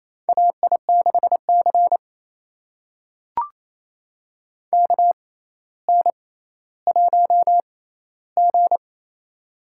Morse code, heard as a keyed tone, about 28 wpm, spaced as for 14 wpm, 700 Hz. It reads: AI6C KN1G
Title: USA Call Signs - 4 Characters - 6x Character Spacing - 28wpm